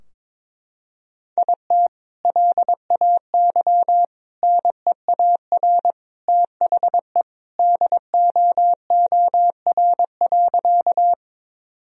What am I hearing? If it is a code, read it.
Morse code, heard as a keyed tone, about 22 wpm, 700 Hz. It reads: IT LAY NEAR THE DOOR.
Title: Sentences from Top 1000 Words - Review Part 2 - 22wpm